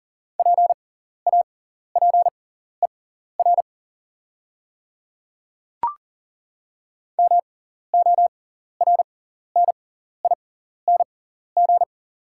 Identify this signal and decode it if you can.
Morse code, heard as a keyed tone, about 40 wpm, spaced as for 14 wpm, 700 Hz. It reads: PAPER MORNING
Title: 6x Character Spacing - Top 500 Words - 40wpm